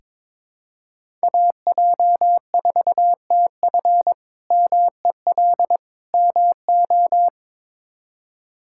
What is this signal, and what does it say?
Morse code, heard as a keyed tone, about 22 wpm, 700 Hz. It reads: AJ4TF MEL MO
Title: K1USN SST QSO Practice - 22wpm